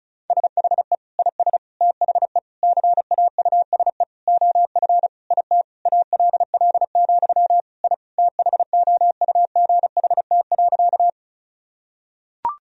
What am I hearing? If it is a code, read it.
Morse code, heard as a keyed tone, about 36 wpm, 700 Hz. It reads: SHE IS THE CAUSE OF IT ALL, I THOUGHT.
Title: Sentences from Top 1000 Words - Review Part 1 - 35wpm